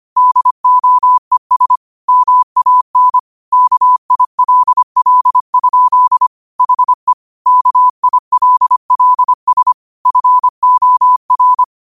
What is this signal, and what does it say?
Morse code, heard as a keyed tone, about 25 wpm, 1000 Hz. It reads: DOES MAN KILL? HE KILLS FOR
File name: qotdfiles / 25wpm